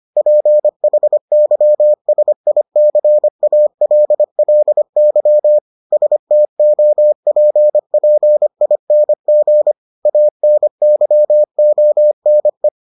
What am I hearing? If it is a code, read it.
Morse code, heard as a keyed tone, about 25 wpm, 600 Hz. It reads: PHYSICALLY STOPPING ANYONE